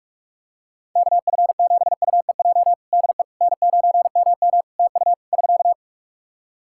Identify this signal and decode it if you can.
Morse code, heard as a keyed tone, about 48 wpm, 700 Hz. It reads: KF7UEJ DE N9MM TU <SK>
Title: Practice QSO - Exchange 4 - Ending - 45wpm